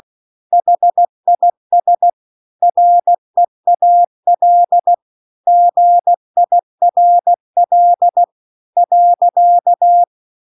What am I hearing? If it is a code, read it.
Morse code, heard as a keyed tone, about 16 wpm, 700 Hz. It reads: HIS REAL GIRL .